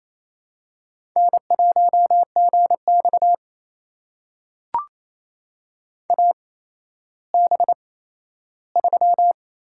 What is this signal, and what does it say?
Morse code, heard as a keyed tone, about 28 wpm, spaced as for 14 wpm, 700 Hz. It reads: N1GX AB3
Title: USA Call Signs - 4 Characters - 8x Character Spacing - 28wpm